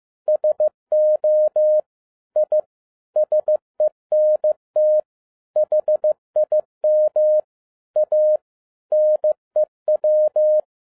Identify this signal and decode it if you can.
Morse code, heard as a keyed tone, about 15 wpm, 610 Hz. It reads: SO I SENT HIM A NEW
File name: 2011-10-mp3 / n07-A DOT 15wpm.mp3